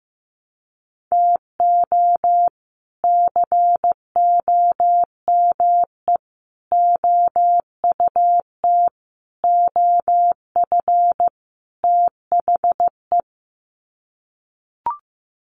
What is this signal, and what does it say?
Morse code, heard as a keyed tone, about 15 wpm, 700 Hz. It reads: TO COME OUT OF THE
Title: Sets of 5 Words Encore2 15wpm